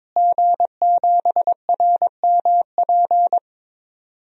Morse code, heard as a keyed tone, about 21 wpm, 700 Hz. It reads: G7RMP
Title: DXCC-100 Call Signs - 5 Characters - Repeated 3x - 22wpm